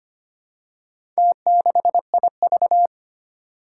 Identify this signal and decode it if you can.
Morse code, heard as a keyed tone, about 25 wpm, 700 Hz. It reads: T6IV
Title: Four Challenging Letter-Number Combinations 25wpm